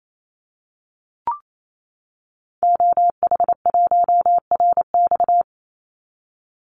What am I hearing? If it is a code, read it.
Morse code, heard as a keyed tone, about 28 wpm, 700 Hz. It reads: OH1RX
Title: Call Signs - 28wpm